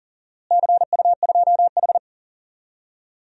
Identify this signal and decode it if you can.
Morse code, heard as a keyed tone, about 40 wpm, 700 Hz. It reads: CU2H
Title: DXCC-100 Call Signs - 4 Characters - Repeated 3x - 40wpm